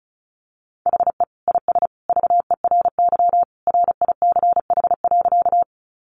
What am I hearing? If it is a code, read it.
Morse code, heard as a keyed tone, about 34 wpm, 700 Hz. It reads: HE IS VERY RICH.
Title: Sentences from Top 1100 Words 35wpm